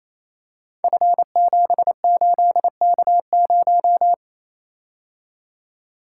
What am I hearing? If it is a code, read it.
Morse code, heard as a keyed tone, about 27 wpm, 700 Hz. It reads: F78K0
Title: Code Groups - 5 Letter:Numbers - 28wpm